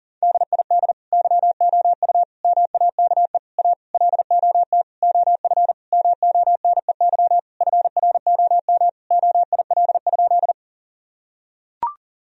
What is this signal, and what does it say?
Morse code, heard as a keyed tone, about 40 wpm, 700 Hz. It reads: DID YOU MAKE A LOT OF MONEY FROM OIL?
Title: Sentences from Top 1000 Words - Review Part 4 - 40wpm